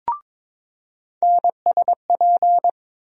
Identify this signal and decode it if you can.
Morse code, heard as a keyed tone, about 22 wpm, 700 Hz. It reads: NSP
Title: USA Call Sign Postfixes - 3 Letters 22wpm